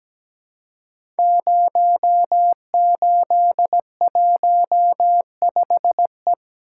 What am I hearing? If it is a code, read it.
Morse code, heard as a keyed tone, about 17 wpm, 700 Hz. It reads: 0815E
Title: Code Groups - 5 Letter:Numbers - 17wpm